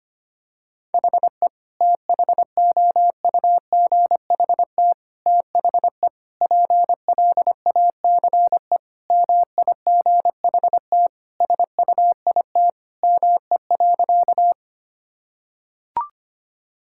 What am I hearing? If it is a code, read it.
Morse code, heard as a keyed tone, about 25 wpm, 700 Hz. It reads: HE THOUGHT THE PLACE MIGHT SUIT ME.
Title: Sentences from Top 1600 Words 25wpm